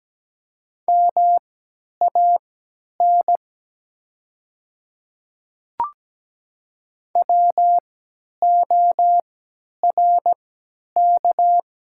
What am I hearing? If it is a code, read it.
Morse code, heard as a keyed tone, about 17 wpm, spaced as for 10 wpm, 700 Hz. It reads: MAN WORK
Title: 3x Character Spacing - Top 200 Words - 17wpm